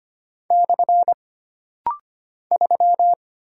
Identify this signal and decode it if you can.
Morse code, heard as a keyed tone, about 25 wpm, 700 Hz. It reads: / 3
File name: Single Character (Letter:Number:Punctuation:Prosign) - Rapid-Fire - 25wpm